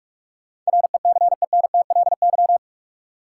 Morse code, heard as a keyed tone, about 46 wpm, 700 Hz. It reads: RECENTLY
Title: Sentences from Top 1200 Words - New Word Drill - 45wpm